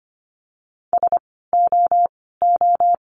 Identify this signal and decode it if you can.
Morse code, heard as a keyed tone, about 25 wpm, 700 Hz. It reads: S O O
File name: course / 25:10wpm / 094 Lesson 16f - Letter c - Words 25wpm